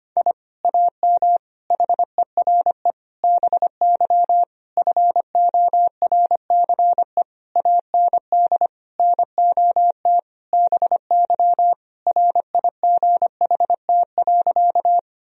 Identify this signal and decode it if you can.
Morse code, heard as a keyed tone, about 25 wpm, 700 Hz. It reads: I AM HERE BY FORCE AND NOT BY RIGHT.
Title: Sentences from Top 1000 Words - Review Part 3 - 25wpm